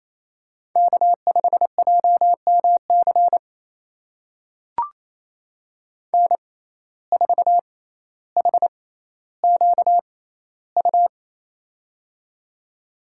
Morse code, heard as a keyed tone, about 28 wpm, spaced as for 14 wpm, 700 Hz. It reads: K5JMC N4HQU
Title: USA Call Signs - 5 Characters - 6x Character Spacing - 28wpm